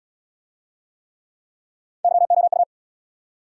Morse code, heard as a keyed tone, about 74 wpm, 690 Hz. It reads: YOU